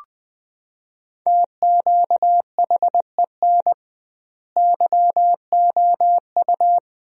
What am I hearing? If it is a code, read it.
Morse code, heard as a keyed tone, about 20 wpm, 700 Hz. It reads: TQHEN YOU
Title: 2x Word-Spacing - Sentences from Top 100 Words - 20wpm